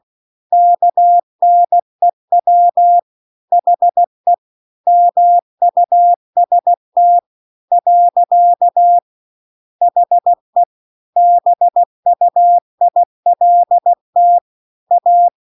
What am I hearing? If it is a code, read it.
Morse code, heard as a keyed tone, about 16 wpm, 700 Hz. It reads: KNEW HE MUST . HE BUILT A
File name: CW-The-Big-Twoheaded-River-2-16WPM